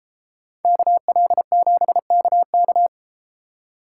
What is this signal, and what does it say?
Morse code, heard as a keyed tone, about 30 wpm, 700 Hz. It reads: KL7KK